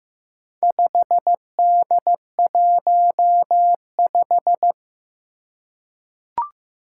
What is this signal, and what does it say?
Morse code, heard as a keyed tone, about 15 wpm, 700 Hz. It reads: 5D15
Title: Four Challenging Letter-Number Combinations 15wpm